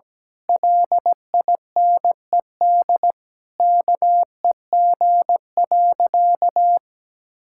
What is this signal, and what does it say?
Morse code, heard as a keyed tone, about 17 wpm, 700 Hz. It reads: LINED KEG.
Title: Pangram Challenge v2 - 17wpm